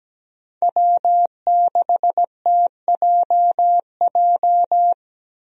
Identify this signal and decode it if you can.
Morse code, heard as a keyed tone, about 17 wpm, 700 Hz. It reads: W6TJJ
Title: USA Call Signs - 5 Characters - Repeated 3x - 17wpm